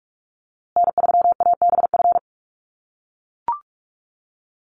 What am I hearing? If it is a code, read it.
Morse code, heard as a keyed tone, about 44 wpm, 700 Hz. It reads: N3ABF
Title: USA Call Signs Repeated 2x 45wpm